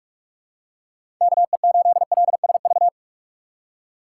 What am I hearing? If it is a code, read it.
Morse code, heard as a keyed tone, about 46 wpm, 700 Hz. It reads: KE8LSV
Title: USA Call Signs - 6 Characters - 3x Character Spacing - 45wpm